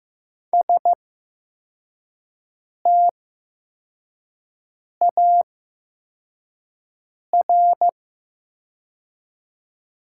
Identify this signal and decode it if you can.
Morse code, heard as a keyed tone, about 15 wpm, spaced as for 4 wpm, 700 Hz. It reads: STAR
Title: 8x Character Spacing - Top 500 Words - 15wpm